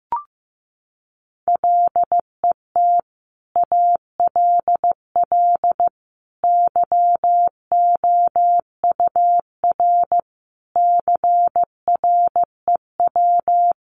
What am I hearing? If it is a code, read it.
Morse code, heard as a keyed tone, about 15 wpm, 700 Hz. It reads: LET ALL YOUR CREW
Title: Sentences from Top 1500 Words 15wpm